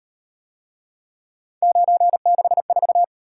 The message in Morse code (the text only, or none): 964